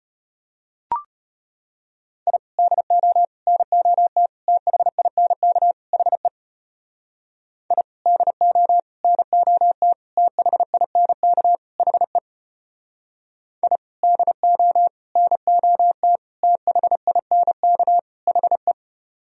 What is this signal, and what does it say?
Morse code, heard as a keyed tone, about 34 wpm, 700 Hz. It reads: I DO NOT THINK HE I DO NOT THINK HE I DO NOT THINK HE